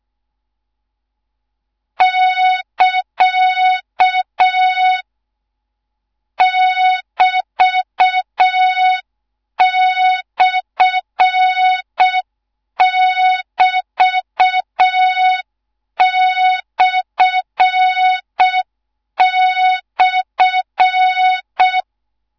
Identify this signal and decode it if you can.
Morse code, heard as a keyed tone, about 6 wpm, 3700 Hz, with noise. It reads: <KA> <BT>/<BT>//
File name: Morze-HA2MN / hang / 13IRASJELEK1 HA2MN